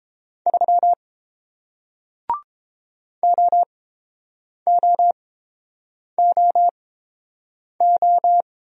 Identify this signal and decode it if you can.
Morse code, heard as a keyed tone, about 30 wpm, spaced as for 7 wpm, 700 Hz. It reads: 3 OOOO